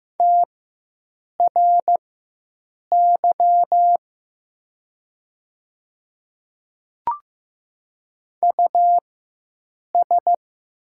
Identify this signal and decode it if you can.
Morse code, heard as a keyed tone, about 15 wpm, spaced as for 7 wpm, 700 Hz. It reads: TRY US
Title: 4x Character Spacing - Top 200 Words - 15wpm